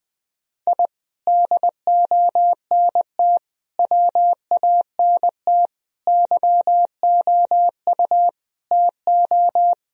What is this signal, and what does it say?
Morse code, heard as a keyed tone, about 20 wpm, 700 Hz. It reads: I DONT WANT YOU TO
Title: Sentences from Top 1000 Words - Review Part 3 - 20wpm